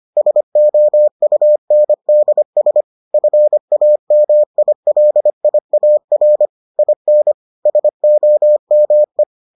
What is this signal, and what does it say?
Morse code, heard as a keyed tone, about 25 wpm, 600 Hz. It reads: SOUNDS FAMILIAR IN SOME